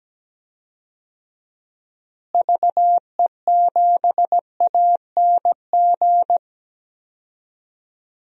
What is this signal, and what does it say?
Morse code, heard as a keyed tone, about 17 wpm, 700 Hz. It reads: VE7ANG
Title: DXCC-100 Call Signs - 6 Characters - Repeated 3x - 17wpm